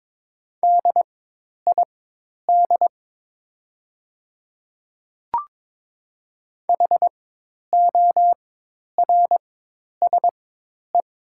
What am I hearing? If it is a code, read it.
Morse code, heard as a keyed tone, about 22 wpm, spaced as for 10 wpm, 700 Hz. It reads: DID HORSE
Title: 4x Character Spacing - Top 300 Words - 22wpm